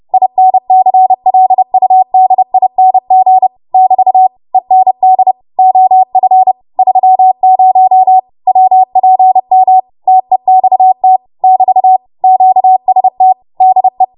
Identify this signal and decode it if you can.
Morse code, heard as a keyed tone, about 30 wpm, 750 Hz. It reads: INCLUDING <BT> END OF 30 WPM TEXT <BT> QST DE
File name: Morse / 30WPM